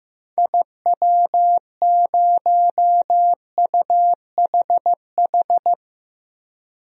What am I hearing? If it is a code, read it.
Morse code, heard as a keyed tone, about 15 wpm, 700 Hz. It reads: IW0UHH
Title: DXCC-100 Call Signs - 6 Characters - 15wpm